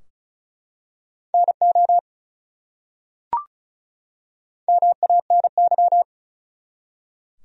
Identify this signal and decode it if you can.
Morse code, heard as a keyed tone, about 33 wpm, 700 Hz. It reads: NO MANY
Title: Top 100 Words 35wpm